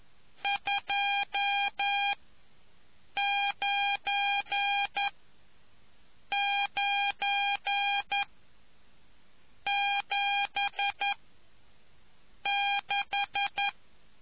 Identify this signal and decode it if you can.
Morse code, heard as a keyed tone, about 11 wpm, spaced as for 5 wpm, 800 Hz. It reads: 29976